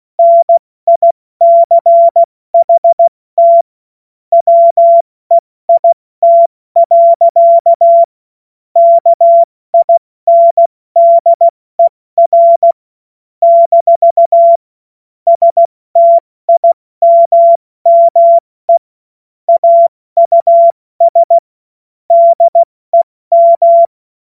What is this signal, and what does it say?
Morse code, heard as a keyed tone, about 16 wpm, 680 Hz. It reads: NICHT WEIT. KINDER - STIMME AUS DEM
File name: Morsekurs / Witze / Witz 26. Oktober 2021, 70 F80